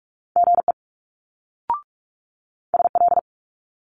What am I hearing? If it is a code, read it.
Morse code, heard as a keyed tone, about 46 wpm, 700 Hz. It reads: GE SL
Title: Multi-letter Phonemes Quick Fire 45wpm